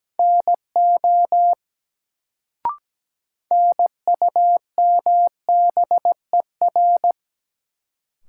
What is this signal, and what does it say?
Morse code, heard as a keyed tone, about 17 wpm, 700 Hz. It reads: NO NUMBER
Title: Sets of 2 Words - New Word Drill - 17wpm